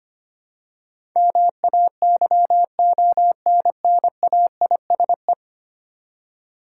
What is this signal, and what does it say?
Morse code, heard as a keyed tone, about 25 wpm, 700 Hz. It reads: MAYONNAISE
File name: practice / 25wpm / Easy Words with Hints - 25wpm